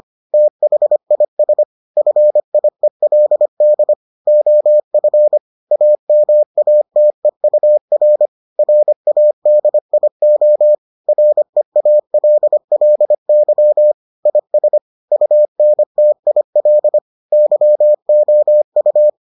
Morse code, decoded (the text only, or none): THIS FIELD OF AMATEUR RADIO REALLY IS UNTIL YOU